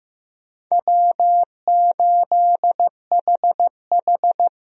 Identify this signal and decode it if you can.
Morse code, heard as a keyed tone, about 16 wpm, 700 Hz. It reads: W8HH
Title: USA Call Signs - 4 Characters - Repeated 3x - 15wpm